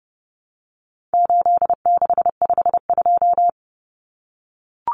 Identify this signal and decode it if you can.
Morse code, heard as a keyed tone, about 30 wpm, 700 Hz. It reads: 8652
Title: Common Cut Numbers - 4-Digit 30wpm